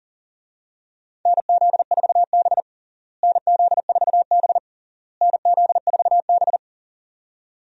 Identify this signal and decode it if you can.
Morse code, heard as a keyed tone, about 40 wpm, 700 Hz. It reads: NZ4B NZ4B NZ4B
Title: USA Call Signs - 4 Characters - Repeated 3x - 40wpm